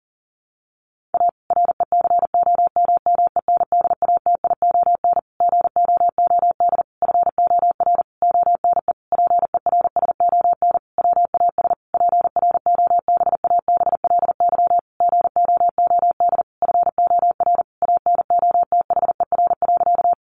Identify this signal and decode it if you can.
Morse code, heard as a keyed tone, about 40 wpm, 700 Hz. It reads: A RECOMMENDATION GOOD FOR ONE PERSON WAS PROBABLY GOOD FOR ANOTHER.
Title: Sentences from Top 1400 Words 40wpm